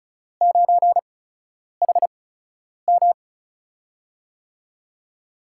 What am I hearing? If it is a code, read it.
Morse code, heard as a keyed tone, about 36 wpm, spaced as for 10 wpm, 700 Hz. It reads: 9HM
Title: USA Call Signs - 5 Characters - 8x Character Spacing - 35wpm